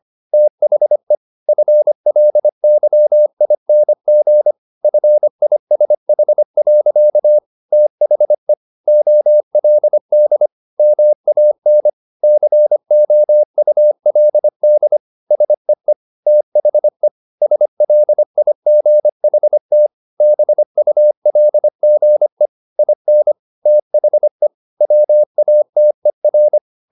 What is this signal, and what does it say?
Morse code, heard as a keyed tone, about 25 wpm, 600 Hz. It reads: THE FLYING FISH. THE OLD MAN COULD SEE THE SLIGHT BULGE IN THE WATER
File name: the old man and the sea part 1 25wpm